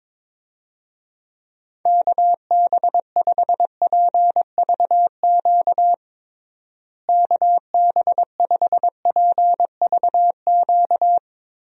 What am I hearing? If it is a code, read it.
Morse code, heard as a keyed tone, about 22 wpm, 700 Hz. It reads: KB5PVQ KB5PVQ
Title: USA Call Signs - 6 Characters - Repeated 3x - 22wpm